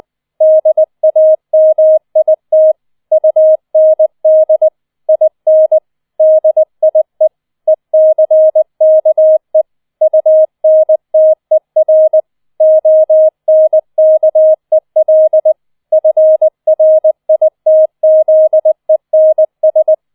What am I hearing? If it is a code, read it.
Morse code, heard as a keyed tone, about 20 wpm, 630 Hz. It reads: DAMIT UND IN DIE ECKE UNTER ONKEL FRITZENS
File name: 20wpm